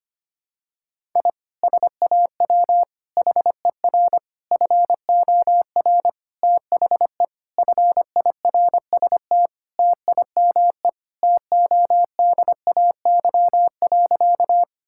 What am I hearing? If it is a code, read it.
Morse code, heard as a keyed tone, about 25 wpm, 700 Hz. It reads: I SAW HER FOR THE FIRST TIME TODAY.
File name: practice / 25wpm / Sentences with 4-Word Phrases - 25wpm